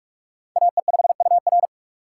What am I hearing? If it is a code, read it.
Morse code, heard as a keyed tone, about 46 wpm, 700 Hz. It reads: AE5UR
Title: USA Call Signs - 5 Characters - 2x Character Spacing - 45wpm